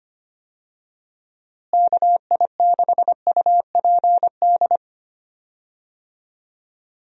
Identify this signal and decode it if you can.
Morse code, heard as a keyed tone, about 25 wpm, 700 Hz. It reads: KI6UPD
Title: USA Call Signs - 6 Characters - Repeated 3x - 25wpm